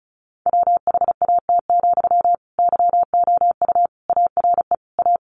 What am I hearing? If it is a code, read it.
Morse code, heard as a keyed tone, about 34 wpm, 700 Hz. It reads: WHAT, YOU ARE A
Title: Sentences from Top 1300 Words 35wpm